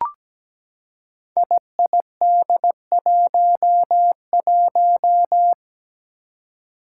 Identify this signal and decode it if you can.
Morse code, heard as a keyed tone, about 17 wpm, 700 Hz. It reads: IID11
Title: Five Challenging Letter-Number Combinations 17wpm